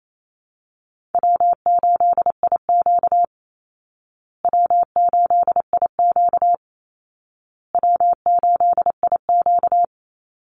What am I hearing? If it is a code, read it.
Morse code, heard as a keyed tone, about 28 wpm, 700 Hz. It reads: W8IQ W8IQ W8IQ
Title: Call Signs Repeated 2x - 28wpm